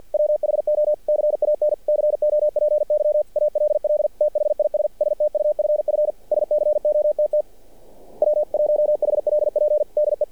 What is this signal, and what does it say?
Morse code, heard as a keyed tone, about 48 wpm, 600 Hz. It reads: WHO CAN COPY ALL THIS STUFF SCOTT W1HLP DE
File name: archived / CRSnet